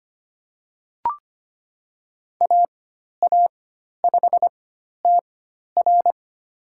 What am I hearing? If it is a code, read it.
Morse code, heard as a keyed tone, about 25 wpm, spaced as for 12 wpm, 700 Hz. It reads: AA5TR
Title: USA Call Signs - 5 Characters - 4x Character Spacing - 25wpm